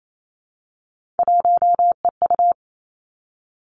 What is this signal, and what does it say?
Morse code, heard as a keyed tone, about 28 wpm, 700 Hz. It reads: 1EU
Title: Three Challenging Letter-Number Combinations 28wpm